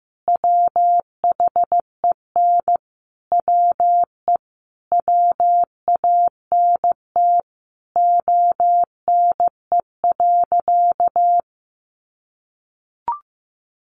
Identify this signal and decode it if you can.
Morse code, heard as a keyed tone, about 15 wpm, 700 Hz. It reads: WHEN WE WANT ONE.
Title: Sentences from Top 1200 Words 15wpm